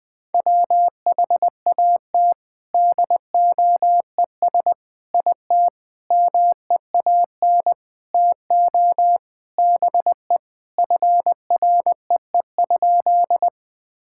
Morse code, heard as a keyed tone, about 20 wpm, 700 Hz. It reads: WHAT DOES IT MEAN TO BE FREE?
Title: Sentences with 6-Word Phrases - 20wpm